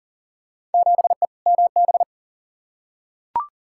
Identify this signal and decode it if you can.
Morse code, heard as a keyed tone, about 40 wpm, 700 Hz. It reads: 7E MB
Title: Field Day Exchanges Encore - 40wpm